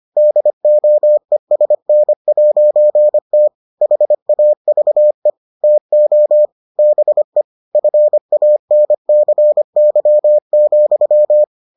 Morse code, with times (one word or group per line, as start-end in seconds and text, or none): 0.2-3.5 DOESN'T
3.8-5.3 HAVE
5.6-6.5 TO
6.8-7.4 BE
7.8-11.4 FANCY,